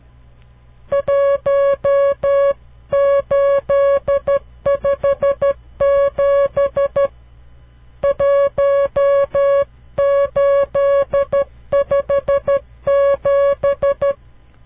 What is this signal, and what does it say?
Morse code, heard as a keyed tone, about 12 wpm, 560 Hz. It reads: 1857 1857
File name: CW Learning Programs / farnsworth / unit 5